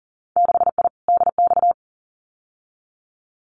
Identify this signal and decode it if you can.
Morse code, heard as a keyed tone, about 40 wpm, 700 Hz. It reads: 6I DX
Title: Winter Field Day Exchanges Encore 40wpm